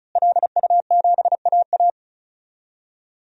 Morse code, heard as a keyed tone, about 36 wpm, 700 Hz. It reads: LU7AA
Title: DXCC-100 Call Signs - 5 Characters - Repeated 3x - 35wpm